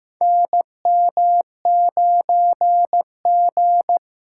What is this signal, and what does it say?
Morse code, heard as a keyed tone, about 15 wpm, 700 Hz. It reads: NM9G
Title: USA Call Signs - 4 Characters - 6x Character Spacing - 15wpm